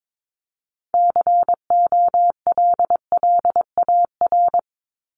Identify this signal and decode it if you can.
Morse code, heard as a keyed tone, about 22 wpm, 700 Hz. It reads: COLLAR
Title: Top 1751-2000 Words 22wpm